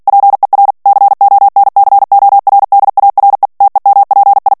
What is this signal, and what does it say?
Morse code, heard as a keyed tone, about 48 wpm, 800 Hz. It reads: PER CONCORDARE TEMPI
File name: morse mp3 / ITALIEN texte clair 1 vitesse 45 mots minute